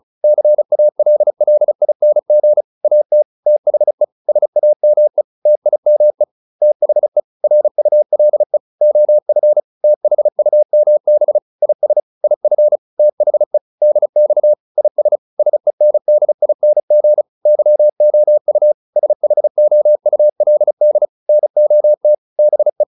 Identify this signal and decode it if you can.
Morse code, heard as a keyed tone, about 34 wpm, 600 Hz. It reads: CALLING AT THE SAME TIME THE RULE OF THUMB IS IF THE DX IS SENDING YOU SHOULD NOT BE